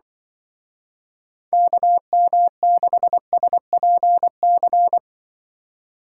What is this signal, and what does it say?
Morse code, heard as a keyed tone, about 24 wpm, 700 Hz. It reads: KM6SPC
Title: Speed Racing - USA Call Signs 20wpm